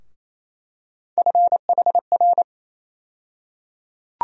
Morse code, heard as a keyed tone, about 28 wpm, 700 Hz. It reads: FHR
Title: USA Call Sign Postfixes - 3 Letters 28wpm